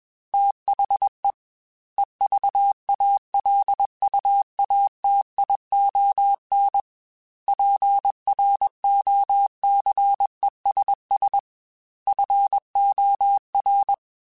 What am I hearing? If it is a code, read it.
Morse code, heard as a keyed tone, about 21 wpm, 800 Hz. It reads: THE EVALUATION PROCESS FOR
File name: Technical Subjects / Technical Subjects-mp3s / AND Gate @20